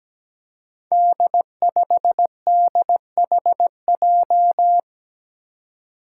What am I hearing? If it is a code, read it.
Morse code, heard as a keyed tone, about 17 wpm, 700 Hz. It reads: D5DHJ
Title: Five Challenging Letter-Number Combinations 17wpm